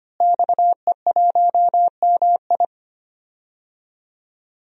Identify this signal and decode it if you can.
Morse code, heard as a keyed tone, about 25 wpm, 700 Hz. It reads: XE1MI
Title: DXCC-100 Call Signs - 5 Characters - Repeated 2x - 25wpm